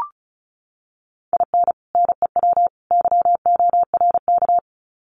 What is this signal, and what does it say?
Morse code, heard as a keyed tone, about 34 wpm, 700 Hz. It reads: IN NEW YORK
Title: Sets of 3 Words 35wpm